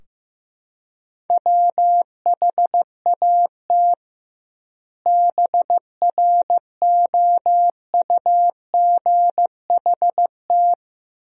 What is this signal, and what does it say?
Morse code, heard as a keyed tone, about 15 wpm, 700 Hz. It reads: WHAT BROUGHT
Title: 2x Word-Spacing - Sentences from Top 500 Words - 15wpm